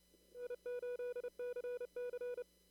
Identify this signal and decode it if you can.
Morse code, heard as a keyed tone, about 29 wpm, 470 Hz. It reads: I8CC
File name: CW / kt5j wpx cw-op-k5pi